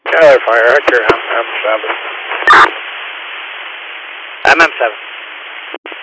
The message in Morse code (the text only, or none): none